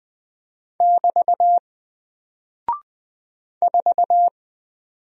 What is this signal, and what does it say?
Morse code, heard as a keyed tone, about 20 wpm, 700 Hz. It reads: <BT> 4
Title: Single Character (Letter:Number:Punctuation:Prosign) 20wpm